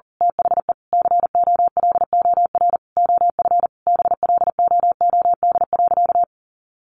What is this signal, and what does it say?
Morse code, heard as a keyed tone, about 40 wpm, 700 Hz. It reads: THE COLOR OF BLOOD.
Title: A Sound Of Thunder 40wpm